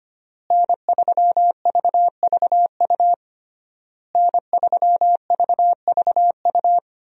Morse code, heard as a keyed tone, about 25 wpm, 700 Hz. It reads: N3VVU N3VVU
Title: USA Call Signs - 5 Characters - Repeated 3x - 25wpm